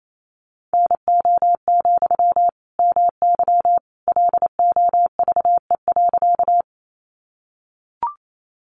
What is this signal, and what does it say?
Morse code, heard as a keyed tone, about 28 wpm, 700 Hz. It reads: NO, MY LOVE.
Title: Sentences from Top 400 Words 28wpm